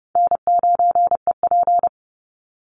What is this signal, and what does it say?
Morse code, heard as a keyed tone, about 29 wpm, 700 Hz. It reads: N9EP
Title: Easy Call Signs Repeated 3x - 30wpm